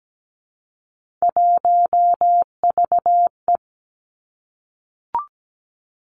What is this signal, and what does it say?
Morse code, heard as a keyed tone, about 17 wpm, 700 Hz. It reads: 1VE
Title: Three Challenging Letter-Number Combinations 17wpm